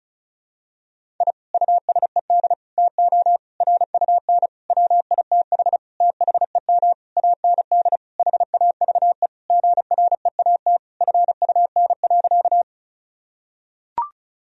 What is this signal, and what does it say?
Morse code, heard as a keyed tone, about 34 wpm, 700 Hz. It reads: I USED TO RUN WITH THEM AND HAVE GREAT FUN.
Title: Sentences from Top 1000 Words - Review Part 4 - 35wpm